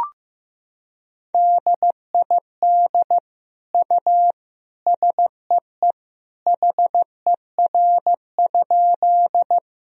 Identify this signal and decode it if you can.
Morse code, heard as a keyed tone, about 15 wpm, 700 Hz. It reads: DID U SEE HER?
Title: Sentences from Top 100 Words and Abbreviations 15wpm